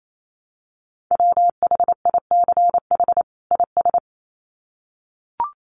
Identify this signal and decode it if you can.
Morse code, heard as a keyed tone, about 28 wpm, 700 Hz. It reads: WHICH IS
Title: Sets of 2 Words 28wpm